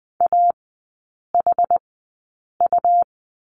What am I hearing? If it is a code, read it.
Morse code, heard as a keyed tone, about 20 wpm, spaced as for 8 wpm, 700 Hz. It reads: AHU
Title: 100 Lesson 17e - Letter u - Three Characters 20wpm